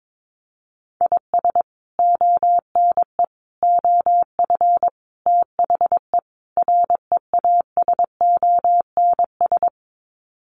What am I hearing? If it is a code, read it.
Morse code, heard as a keyed tone, about 22 wpm, 700 Hz. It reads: IS ONE OF THE REASONS
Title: Sets of 5 Words Encore2 22wpm